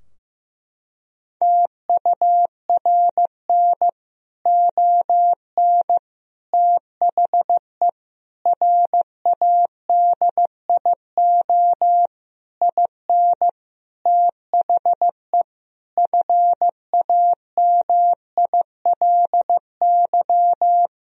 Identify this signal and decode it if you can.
Morse code, heard as a keyed tone, about 15 wpm, 700 Hz. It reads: TURN ON THE RADIO IN THE FAMILY
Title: Sentences from Top 1000 Words - Review Part 3 - 15wpm